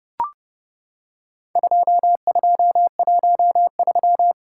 Morse code, heard as a keyed tone, about 30 wpm, 700 Hz. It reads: 2213
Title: Worked All Europe DX Contest - QTC Exchange - 30wpm